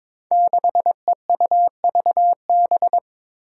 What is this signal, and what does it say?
Morse code, heard as a keyed tone, about 22 wpm, 700 Hz. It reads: 6EUVB
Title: Five Challenging Letter-Number Combinations 22wpm